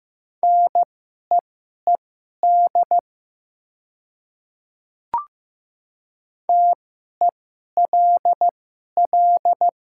Morse code, heard as a keyed tone, about 15 wpm, spaced as for 11 wpm, 700 Hz. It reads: NEED TELL
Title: 2x Character Spacing - Top 200 Words - 15wpm